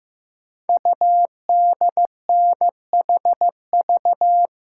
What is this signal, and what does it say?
Morse code, heard as a keyed tone, about 16 wpm, 700 Hz. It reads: UDNHV
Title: Code Groups - 5 Letters - 15wpm